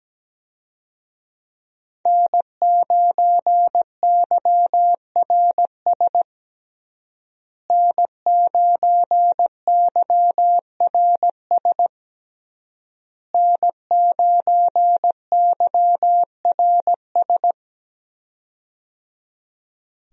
Morse code, heard as a keyed tone, about 17 wpm, 700 Hz. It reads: N9YRS N9YRS N9YRS
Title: USA Call Signs - 5 Characters - Repeated 3x - 17wpm